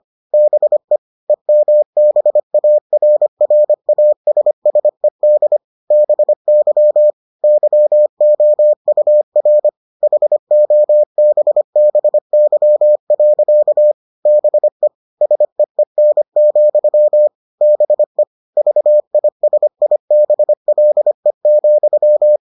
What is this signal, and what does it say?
Morse code, heard as a keyed tone, about 25 wpm, 600 Hz. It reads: BE EMBARRASSED BY YOUR HOBBY. BE SEEN, BE VISIBLE,